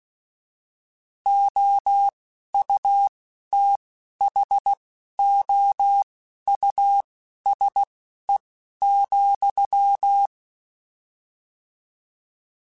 Morse code, heard as a keyed tone, about 16 wpm, spaced as for 11 wpm, 800 Hz. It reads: OUTHOUSE,